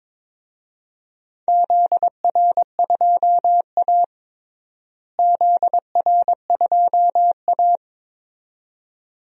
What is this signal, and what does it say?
Morse code, heard as a keyed tone, about 22 wpm, 700 Hz. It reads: ZR2A ZR2A
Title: DXCC-100 Call Signs - 4 Characters - Repeated 2x - 22wpm